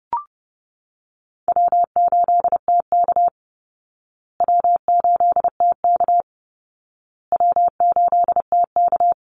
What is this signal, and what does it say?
Morse code, heard as a keyed tone, about 30 wpm, 700 Hz. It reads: W8TK W8TK W8TK
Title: Easy Call Signs Repeated 3x - 30wpm